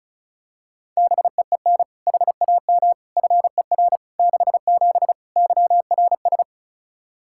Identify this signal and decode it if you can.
Morse code, heard as a keyed tone, about 34 wpm, 700 Hz. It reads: BEEN HAM FER 67 YRS
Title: Practice QSO - Exchange 3 - More Info - 35wpm